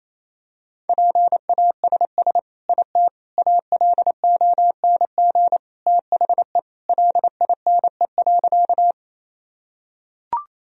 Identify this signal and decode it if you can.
Morse code, heard as a keyed tone, about 28 wpm, 700 Hz. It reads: PASS IT ALONG THE LINE.
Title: Sentences from Top 1000 Words - Review Part 1 - 28wpm